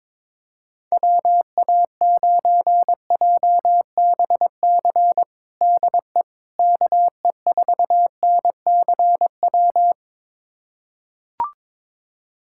Morse code, heard as a keyed tone, about 21 wpm, 700 Hz. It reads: WA9JBC DE KE4NCW
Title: USA Callsign DE Callsign - 6 Characters - 22wpm